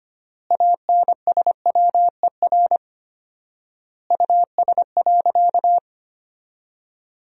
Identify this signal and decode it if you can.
Morse code, heard as a keyed tone, about 25 wpm, 700 Hz. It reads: ANSWER US.
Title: 4x Word-Spacing - Sentences from Top 300 Words - 25wpm